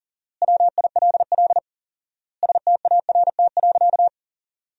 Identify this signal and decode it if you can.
Morse code, heard as a keyed tone, about 37 wpm, 700 Hz. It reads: WILL START.
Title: 4x Word-Spacing - Sentences from Top 300 Words - 40wpm